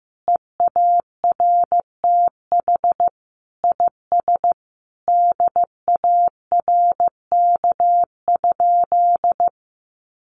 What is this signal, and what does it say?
Morse code, heard as a keyed tone, about 15 wpm, 700 Hz. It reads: EARTH IS DARK?
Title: Sentences from Top 900 Words 15wpm